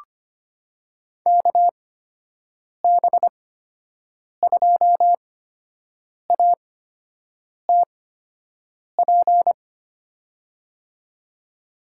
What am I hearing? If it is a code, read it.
Morse code, heard as a keyed tone, about 25 wpm, spaced as for 7 wpm, 700 Hz. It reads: KB2ATP
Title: USA Call Signs - 8x Character Spacing - 25wpm